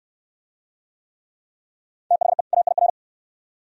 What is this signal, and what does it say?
Morse code, heard as a keyed tone, about 69 wpm, 700 Hz. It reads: THE AIR